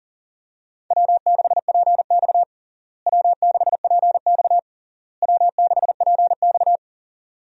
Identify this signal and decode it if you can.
Morse code, heard as a keyed tone, about 40 wpm, 700 Hz. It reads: W6PX W6PX W6PX
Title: USA Call Signs - 4 Characters - Repeated 3x - 40wpm